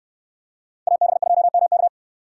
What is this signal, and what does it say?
Morse code, heard as a keyed tone, about 69 wpm, 700 Hz. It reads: AB2MK